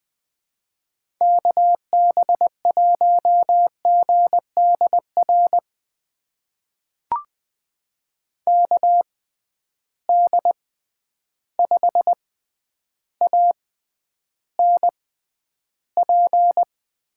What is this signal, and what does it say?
Morse code, heard as a keyed tone, about 20 wpm, spaced as for 10 wpm, 700 Hz. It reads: KB1GDR KD5ANP
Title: USA Call Signs - 6 Characters - 6x Character Spacing - 20wpm